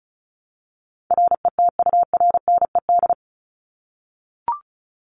Morse code, heard as a keyed tone, about 34 wpm, 700 Hz. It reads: RETURNED
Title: ED Suffix - Top 100 Words with Base Words 35wpm